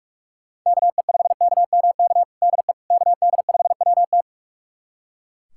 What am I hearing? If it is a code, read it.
Morse code, heard as a keyed tone, about 41 wpm, 700 Hz. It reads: KE5KMK DE KD5WT
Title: USA Callsign DE Callsign 45wpm